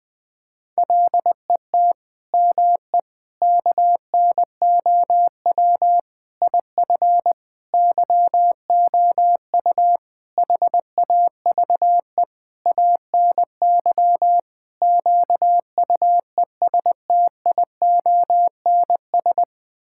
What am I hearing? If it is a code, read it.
Morse code, heard as a keyed tone, about 20 wpm, 700 Hz. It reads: LET ME KNOW IF YOU HAVE ANY QUESTIONS
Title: Sets of 8 Words - 20wpm